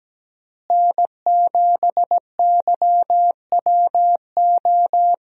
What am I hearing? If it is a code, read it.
Morse code, heard as a keyed tone, about 17 wpm, 700 Hz. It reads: N7YWO
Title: USA Call Signs - 5 Characters - Repeated 3x - 17wpm